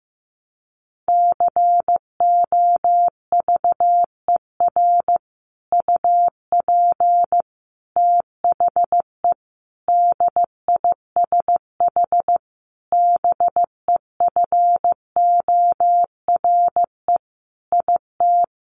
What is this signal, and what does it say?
Morse code, heard as a keyed tone, about 15 wpm, 700 Hz. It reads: COVER UP THE DISH BEFORE IT
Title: Sentences from Top 1300 Words 15wpm